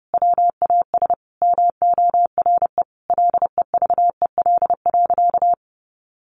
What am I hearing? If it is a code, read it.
Morse code, heard as a keyed tone, about 30 wpm, 700 Hz. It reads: WAS MORE LEVEL.
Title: Sentences from Top 700 Words 30wpm